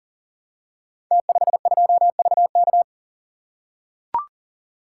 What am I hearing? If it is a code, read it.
Morse code, heard as a keyed tone, about 38 wpm, 700 Hz. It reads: T52VK
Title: Code Groups - 5 Letter:Numbers - 40wpm